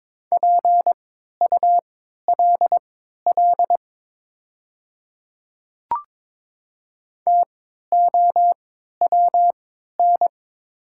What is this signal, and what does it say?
Morse code, heard as a keyed tone, about 22 wpm, spaced as for 12 wpm, 700 Hz. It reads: PULL TOWN
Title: 3x Character Spacing - Top 500 Words - 22wpm